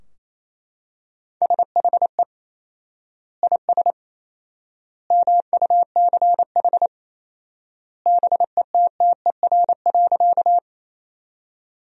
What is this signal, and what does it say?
Morse code, heard as a keyed tone, about 28 wpm, 700 Hz. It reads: SHE IS MUCH BETTER.
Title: 4x Word-Spacing - Sentences from Top 400 Words - 28wpm